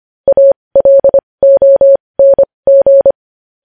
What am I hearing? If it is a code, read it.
Morse code, heard as a keyed tone, about 25 wpm, spaced as for 20 wpm, 550 Hz. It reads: ALONG
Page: news-headlines-in-morse-code-at-20-wpm